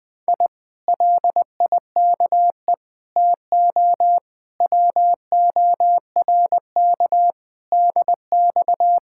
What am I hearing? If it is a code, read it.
Morse code, heard as a keyed tone, about 20 wpm, 700 Hz. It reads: I LIKE TO WORK DX
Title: Speed-Racing Common QSO Phrases Encore - 20wpm